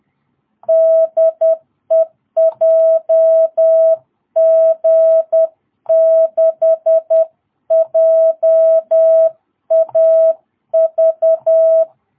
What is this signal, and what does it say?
Morse code, heard as a keyed tone, about 10 wpm, 660 Hz. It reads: DE JG6JAV
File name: jh8cbh / 2022cw